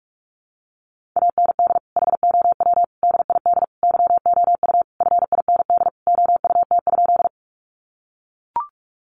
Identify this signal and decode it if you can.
Morse code, heard as a keyed tone, about 44 wpm, 700 Hz. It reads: AND HOW DID YOU FIND OUT?
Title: Sentences from Top 100 Words 45wpm